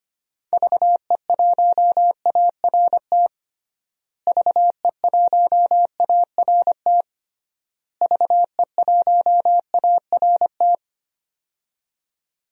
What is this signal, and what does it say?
Morse code, heard as a keyed tone, about 25 wpm, 700 Hz. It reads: VE1ART VE1ART VE1ART
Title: DXCC-100 Call Signs - 6 Characters - Repeated 3x - 25wpm